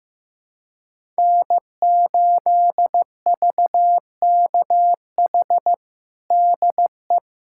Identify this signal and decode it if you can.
Morse code, heard as a keyed tone, about 15 wpm, 700 Hz. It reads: N8VKH DE
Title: USA Callsign DE Callsign - 5 Characters - 15wpm